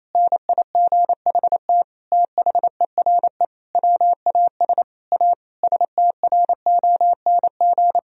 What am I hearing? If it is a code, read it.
Morse code, heard as a keyed tone, about 28 wpm, 700 Hz. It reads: NIGHT THERE WAS A STRONG
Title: Sentences from Top 1000 Words - Review Part 4 - 28wpm